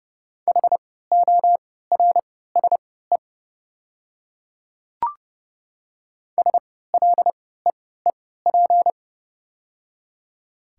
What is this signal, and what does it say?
Morse code, heard as a keyed tone, about 30 wpm, spaced as for 17 wpm, 700 Hz. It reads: HORSE SLEEP
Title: 3x Character Spacing - Top 500 Words - 30wpm